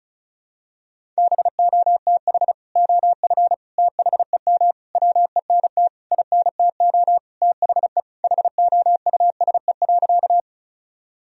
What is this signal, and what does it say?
Morse code, heard as a keyed tone, about 36 wpm, 700 Hz. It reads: BOTH OF THEM WENT INTO THE HOUSE.
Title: Sentences from Top 1000 Words - Review Part 1 - 35wpm